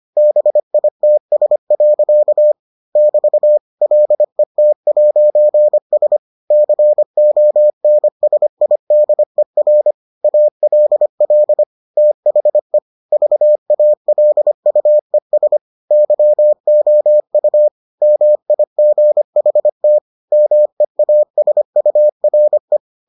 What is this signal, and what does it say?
Morse code, heard as a keyed tone, about 25 wpm, 600 Hz. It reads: BITS. <BT> LET'S CONSIDER ALL THE VALUES YOU MIGHT MEASURE